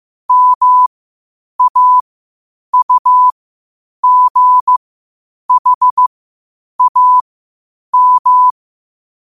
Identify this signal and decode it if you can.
Morse code, heard as a keyed tone, about 15 wpm, spaced as for 8 wpm, 1000 Hz. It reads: MAUGHAM
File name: qotdfiles / 8wpm